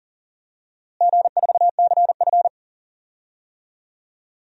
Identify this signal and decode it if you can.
Morse code, heard as a keyed tone, about 40 wpm, 700 Hz. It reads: G4CF
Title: DXCC-100 Call Signs - 4 Characters - Repeated 3x - 40wpm